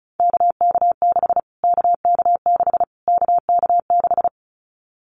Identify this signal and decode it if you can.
Morse code, heard as a keyed tone, about 34 wpm, 700 Hz. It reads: KK6 KK6 KK6
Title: Callsign Prefixes for Top 25 DXCC - Repeated 3x - 35wpm